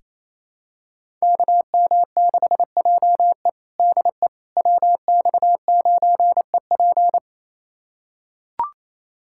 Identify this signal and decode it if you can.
Morse code, heard as a keyed tone, about 28 wpm, 700 Hz. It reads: KM6JE DE WX9EP
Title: USA Callsign DE Callsign 28wpm